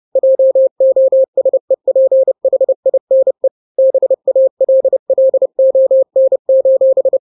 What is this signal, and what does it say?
Morse code, heard as a keyed tone, about 30 wpm, 530 Hz. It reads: JOSEPHINE BALLON:
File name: 29wpm